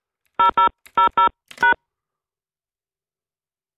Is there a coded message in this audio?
Yes: DTMF from 0.4 to 1.7 s, heard as telephone touch tones, tones of about 120 ms. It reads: 0000#